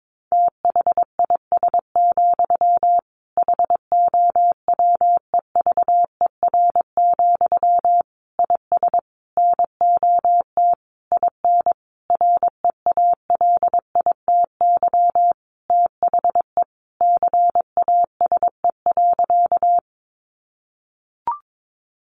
Morse code, heard as a keyed tone, about 22 wpm, 700 Hz. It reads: THIS, HOWEVER, IS NOT IN REALITY THE CASE.
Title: Sentences from Top 1200 Words 22wpm